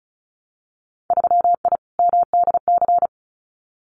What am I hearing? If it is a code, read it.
Morse code, heard as a keyed tone, about 36 wpm, 700 Hz. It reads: 3I MDC
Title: Winter Field Day Exchanges Encore 35wpm